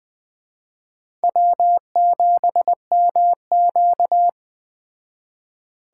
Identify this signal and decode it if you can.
Morse code, heard as a keyed tone, about 20 wpm, 700 Hz. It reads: W7MQ